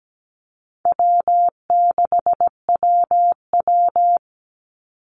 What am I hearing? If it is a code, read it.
Morse code, heard as a keyed tone, about 17 wpm, 700 Hz. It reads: W6WW
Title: NA SOTA Chaser Callsigns 17wpm